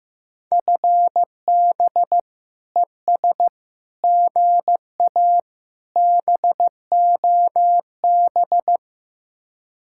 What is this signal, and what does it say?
Morse code, heard as a keyed tone, about 16 wpm, 700 Hz. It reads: FB ES GA BOB
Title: Speed-Racing Common QSO Phrases Encore - 15wpm